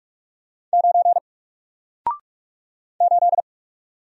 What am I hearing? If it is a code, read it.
Morse code, heard as a keyed tone, about 44 wpm, 700 Hz. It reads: 9 8
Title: Single Letter-Number Alt 45wpm